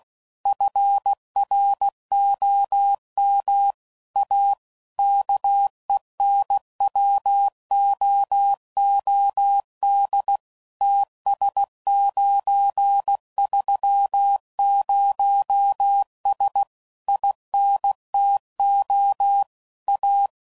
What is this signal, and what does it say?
Morse code, heard as a keyed tone, about 16 wpm, 800 Hz. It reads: FROM A KENWOOD TS930S INTO A
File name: CW Code Groups @15WPM / QSOs / WA7QOX QRX